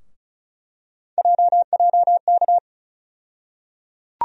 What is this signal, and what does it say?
Morse code, heard as a keyed tone, about 36 wpm, 700 Hz. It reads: JJK
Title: USA Call Sign Postfixes - 3 Letters 35wpm